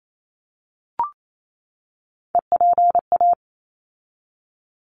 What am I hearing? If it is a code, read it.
Morse code, heard as a keyed tone, about 28 wpm, 700 Hz. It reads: EPA
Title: ARRL Sections 28wpm